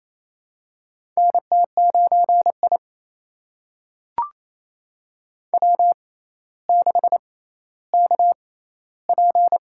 Morse code, heard as a keyed tone, about 28 wpm, spaced as for 14 wpm, 700 Hz. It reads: NT9I W6KP
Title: USA Call Signs - 6x Character Spacing - 28wpm